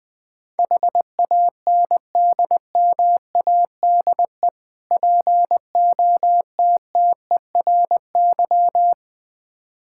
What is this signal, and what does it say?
Morse code, heard as a keyed tone, about 21 wpm, 700 Hz. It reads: HANDMADE POTTERY
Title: Difficult Words with Hints - 20wpm